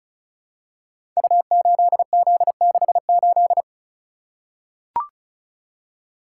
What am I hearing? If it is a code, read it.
Morse code, heard as a keyed tone, about 36 wpm, 700 Hz. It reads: U8Z68
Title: Code Groups - 5 Letter:Numbers - 35wpm